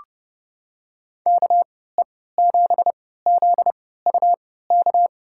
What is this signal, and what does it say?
Morse code, heard as a keyed tone, about 30 wpm, spaced as for 17 wpm, 690 Hz. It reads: KE7ZUK
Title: USA Call Signs - 3x Character Spacing - 30wpm